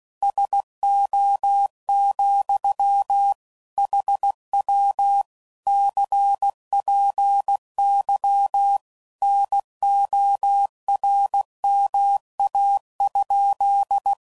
Morse code, heard as a keyed tone, about 16 wpm, 800 Hz. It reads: SO, HW CPY NORMA?
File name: CW Code Groups @15WPM / QSOs / WD0FBT Spaced